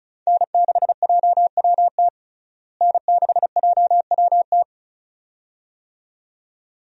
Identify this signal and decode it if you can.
Morse code, heard as a keyed tone, about 36 wpm, 700 Hz. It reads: N6JWT N6JWT
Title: USA Call Signs - 5 Characters - Repeated 2x - 35wpm